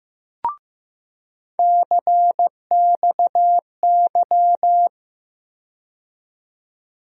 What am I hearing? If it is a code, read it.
Morse code, heard as a keyed tone, about 15 wpm, 700 Hz. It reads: CXY
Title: USA Call Sign Postfixes - 3 Letters 15wpm